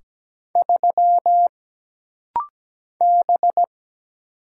Morse code, heard as a keyed tone, about 17 wpm, 700 Hz. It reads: 3 B
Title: Single Letter-Number Alt - Rapid-Fire 17wpm